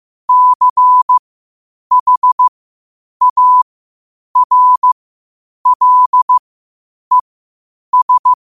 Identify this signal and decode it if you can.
Morse code, heard as a keyed tone, about 15 wpm, spaced as for 8 wpm, 1000 Hz. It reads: CHARLES
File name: qotdfiles / 8wpm